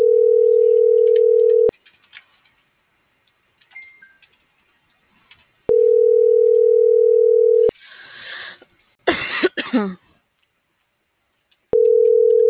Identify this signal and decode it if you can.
Ringback tone (North America), 440 + 480 Hz.